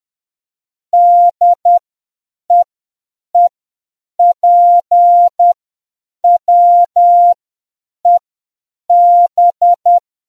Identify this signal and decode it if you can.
Morse code, heard as a keyed tone, about 10 wpm, spaced as for 7 wpm, 700 Hz. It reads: DEEPWEB